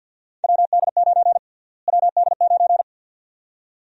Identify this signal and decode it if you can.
Morse code, heard as a keyed tone, about 52 wpm, 700 Hz. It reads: WD9 WD9
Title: USA Call Sign Prefixes - Repeated 2x - 50wpm